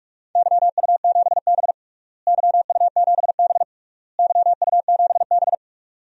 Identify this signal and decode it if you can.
Morse code, heard as a keyed tone, about 41 wpm, 700 Hz. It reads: YU7B YU7B YU7B
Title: DXCC-100 Call Signs - 4 Characters - Repeated 3x - 45wpm